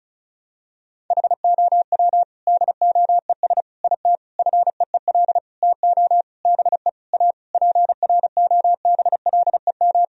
Morse code, heard as a keyed tone, about 34 wpm, 700 Hz. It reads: HOW DOES IT FEEL TO BE A PROBLEM
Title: Sentences from Top 1000 Words - Review Part 2 - 35wpm